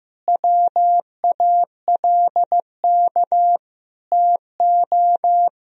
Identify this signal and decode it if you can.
Morse code, heard as a keyed tone, about 15 wpm, 700 Hz. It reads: WALK TO
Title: Sentences from Top 1000 Words - Review Part 1 - 15wpm